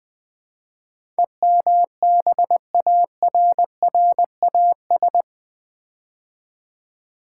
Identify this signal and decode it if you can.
Morse code, heard as a keyed tone, about 20 wpm, 700 Hz. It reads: EMBARRAS
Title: Build A Word - 4 Letters At A Time - 12-Letter Words - 20wpm